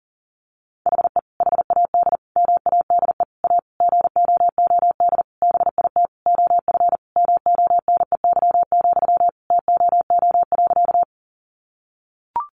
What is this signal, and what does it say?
Morse code, heard as a keyed tone, about 37 wpm, 700 Hz. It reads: HE HAD MADE A GOOD BIT OF MONEY, TOO.
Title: Sentences from Top 900 Words 40wpm